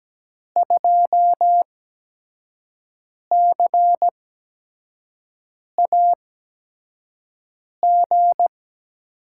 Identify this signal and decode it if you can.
Morse code, heard as a keyed tone, about 17 wpm, spaced as for 5 wpm, 700 Hz. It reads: 2CAG
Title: USA Call Signs - 6 Characters - 8x Character Spacing - 17wpm